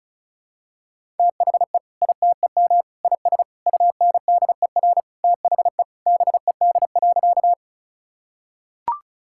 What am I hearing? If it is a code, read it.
Morse code, heard as a keyed tone, about 36 wpm, 700 Hz. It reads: THE ITEM IS UNDER THE BED.
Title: Sentences from Top 1000 Words - Review Part 6 - 35wpm